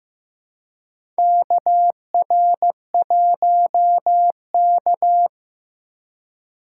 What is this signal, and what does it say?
Morse code, heard as a keyed tone, about 15 wpm, 700 Hz. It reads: KR1K
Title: USA Call Signs - 4 Characters - 3x Character Spacing - 15wpm